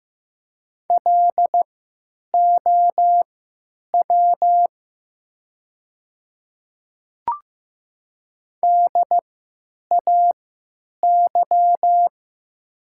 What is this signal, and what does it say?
Morse code, heard as a keyed tone, about 15 wpm, spaced as for 9 wpm, 700 Hz. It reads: LOW DAY
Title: 3x Character Spacing - Top 200 Words - 15wpm